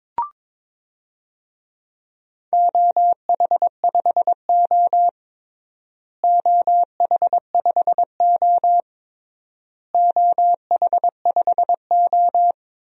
Morse code, heard as a keyed tone, about 22 wpm, 700 Hz. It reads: OH5O OH5O OH5O
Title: DXCC-100 Call Signs - 4 Characters - Repeated 3x - 22wpm